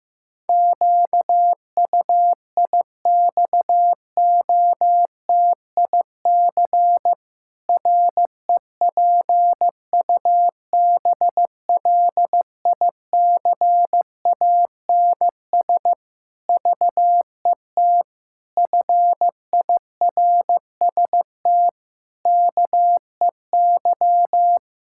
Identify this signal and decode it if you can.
Morse code, heard as a keyed tone, about 16 wpm, 700 Hz. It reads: QUIXOTIC REPUBLICANS VET FIRST KEY
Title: Pangram Challenge v2 - 15wpm